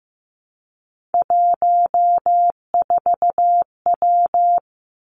Morse code, heard as a keyed tone, about 15 wpm, 700 Hz. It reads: 14W
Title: Three Challenging Letter-Number Combinations 15wpm